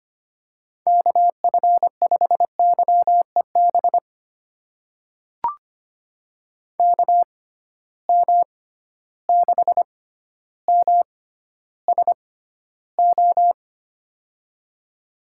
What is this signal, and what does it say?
Morse code, heard as a keyed tone, about 25 wpm, spaced as for 13 wpm, 700 Hz. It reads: KF5YEB KM6MSO
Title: USA Call Signs - 6 Characters - 6x Character Spacing - 25wpm